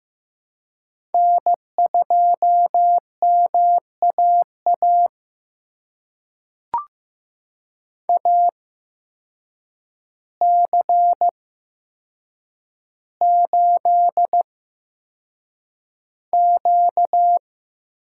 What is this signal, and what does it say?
Morse code, heard as a keyed tone, about 15 wpm, spaced as for 7 wpm, 700 Hz. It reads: N2MAA AC8Q
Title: USA Call Signs - 8x Character Spacing - 15wpm